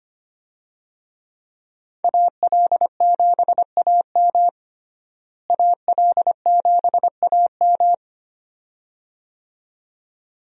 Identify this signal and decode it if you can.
Morse code, heard as a keyed tone, about 25 wpm, 700 Hz. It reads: AL7AM AL7AM
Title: USA Call Signs - 5 Characters - Repeated 2x - 25wpm